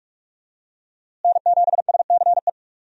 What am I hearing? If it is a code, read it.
Morse code, heard as a keyed tone, about 45 wpm, 700 Hz. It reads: N7SCE
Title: USA Call Signs - 5 Characters - Repeated 3x - 45wpm